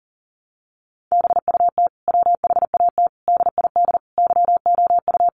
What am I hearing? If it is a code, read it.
Morse code, heard as a keyed tone, about 40 wpm, 700 Hz. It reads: BUT WHAT DID YOU
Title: Sentences from Top 300 Words 40wpm